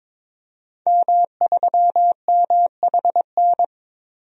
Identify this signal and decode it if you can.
Morse code, heard as a keyed tone, about 22 wpm, 700 Hz. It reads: M3MHN
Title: DXCC-100 Call Signs - 5 Characters - 22wpm